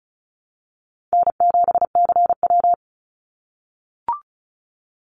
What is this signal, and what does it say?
Morse code, heard as a keyed tone, about 34 wpm, 700 Hz. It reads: N7CW
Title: NA SOTA Chaser Callsigns 35wpm